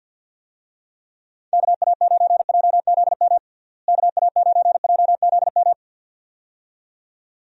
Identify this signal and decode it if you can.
Morse code, heard as a keyed tone, about 47 wpm, 700 Hz. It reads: KA9JZM KA9JZM
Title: USA Call Signs - 6 Characters - Repeated 2x - 50wpm